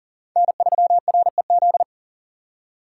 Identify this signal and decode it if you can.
Morse code, heard as a keyed tone, about 51 wpm, 700 Hz. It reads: N3REZ